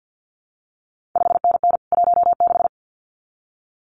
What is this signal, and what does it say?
Morse code, heard as a keyed tone, about 50 wpm, 700 Hz. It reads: 5NN 16
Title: CQ WW DX Contest Exchange - 50wpm